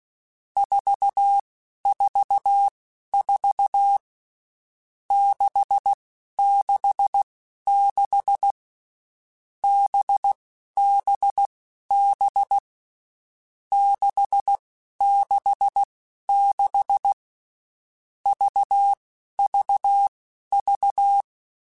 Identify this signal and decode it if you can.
Morse code, heard as a keyed tone, about 16 wpm, spaced as for 11 wpm, 800 Hz. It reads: 444 666 BBB 666 VVV
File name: Groups@10 / Groups@10 mp3s / VB46@10